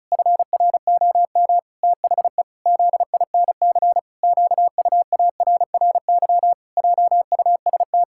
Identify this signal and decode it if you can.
Morse code, heard as a keyed tone, about 36 wpm, 700 Hz. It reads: FROM THE ZINC QUARRY JUST
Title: Pangram Challenge v2 - 35wpm